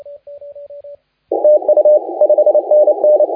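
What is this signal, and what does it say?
Morse code, heard as a keyed tone, about 27 wpm, 590 Hz, with radio noise. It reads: TU 5NN
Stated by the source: Morse contest audio recorded by LA4C during the SAC CW 2018 contest.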